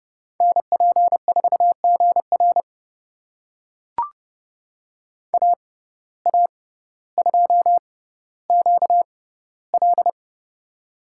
Morse code, heard as a keyed tone, about 30 wpm, spaced as for 15 wpm, 700 Hz. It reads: NP4GR AA2QL
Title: USA Call Signs - 5 Characters - 6x Character Spacing - 30wpm